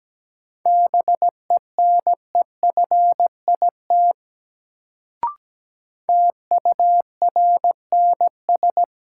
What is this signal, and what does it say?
Morse code, heard as a keyed tone, about 17 wpm, 700 Hz. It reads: BENEFIT TURNS
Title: Sets of 5 Words Encore - New Word Drill - 17wpm